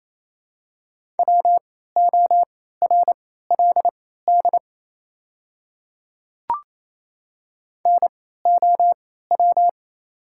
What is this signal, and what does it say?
Morse code, heard as a keyed tone, about 28 wpm, spaced as for 16 wpm, 700 Hz. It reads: WORLD NOW
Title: 3x Character Spacing - Top 500 Words - 28wpm